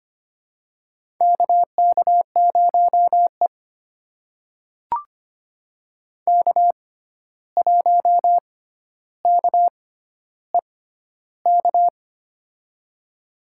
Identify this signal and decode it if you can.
Morse code, heard as a keyed tone, about 25 wpm, spaced as for 12 wpm, 700 Hz. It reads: KK0E K1KEK
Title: USA Call Signs - 6x Character Spacing - 25wpm